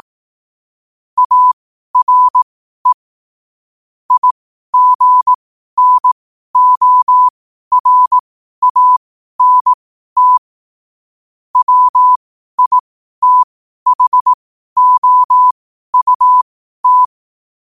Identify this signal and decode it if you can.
Morse code, heard as a keyed tone, about 18 wpm, spaced as for 12 wpm, 1000 Hz. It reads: ARE IGNORANT WITHOUT
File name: qotdfiles / 12wpm